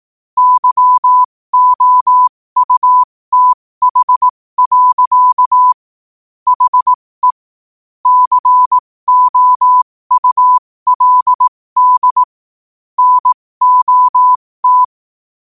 Morse code, heard as a keyed tone, about 18 wpm, spaced as for 15 wpm, 1000 Hz. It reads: YOUTH. HE COULD NOT
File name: qotdfiles / 15wpm